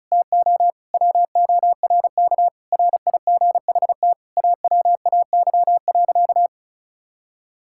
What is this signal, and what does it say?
Morse code, heard as a keyed tone, about 36 wpm, 700 Hz. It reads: TO WORK RIGHT AWAY.
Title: Sentences from Top 1600 Words 35wpm